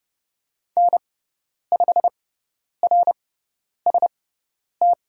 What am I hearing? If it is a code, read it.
Morse code, heard as a keyed tone, about 30 wpm, spaced as for 10 wpm, 700 Hz. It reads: N5RST
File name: Lesson 16g - Letter c - Call Signs 30wpm